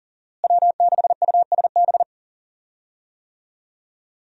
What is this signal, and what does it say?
Morse code, heard as a keyed tone, about 40 wpm, 700 Hz. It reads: W6USB
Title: USA Call Signs - 5 Characters - Repeated 2x - 40wpm